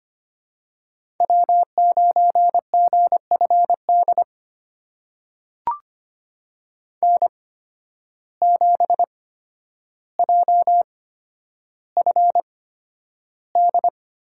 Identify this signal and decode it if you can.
Morse code, heard as a keyed tone, about 25 wpm, spaced as for 11 wpm, 700 Hz. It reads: W9GFD N7JFD
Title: USA Call Signs - 8x Character Spacing - 25wpm